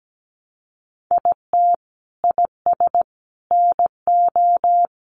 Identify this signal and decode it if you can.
Morse code, heard as a keyed tone, about 17 wpm, 700 Hz. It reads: IT IS NO
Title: Sentences from Top 1200 Words 17wpm